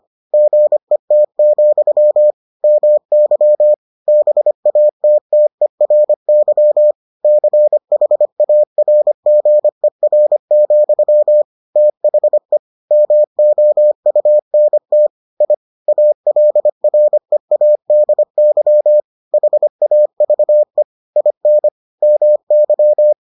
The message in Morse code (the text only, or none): GET, MY BATTERY CHARGER, THE MOUNT I ALREADY HAVE IN MY